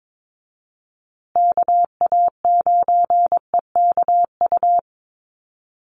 Morse code, heard as a keyed tone, about 22 wpm, 700 Hz. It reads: KA9EKU
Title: USA Call Signs Repeated 2x 22wpm